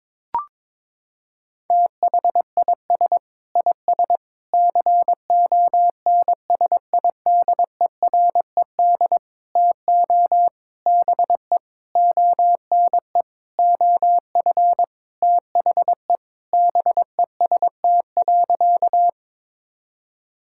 Morse code, heard as a keyed tone, about 22 wpm, 700 Hz. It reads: THIS IS CONSIDERED TO BE ONE OF THE BEST.
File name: practice / 22wpm / Sentences with 7-Word Phrases - 22wpm